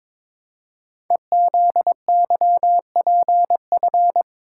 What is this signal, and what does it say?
Morse code, heard as a keyed tone, about 22 wpm, 700 Hz. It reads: EZYPF
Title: Code Groups - 5 Letters - 22wpm